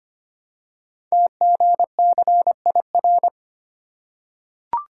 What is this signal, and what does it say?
Morse code, heard as a keyed tone, about 25 wpm, 700 Hz. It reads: TGCIR
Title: Code Groups - 5 Letters - 25wpm